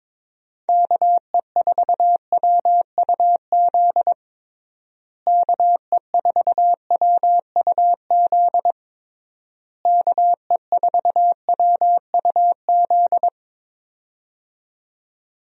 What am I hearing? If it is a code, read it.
Morse code, heard as a keyed tone, about 22 wpm, 700 Hz. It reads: KE4WUZ KE4WUZ KE4WUZ
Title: USA Call Signs - 6 Characters - Repeated 3x - 22wpm